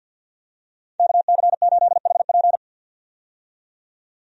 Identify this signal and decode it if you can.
Morse code, heard as a keyed tone, about 49 wpm, 700 Hz. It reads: KC8HP
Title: USA Call Signs - 5 Characters - 50wpm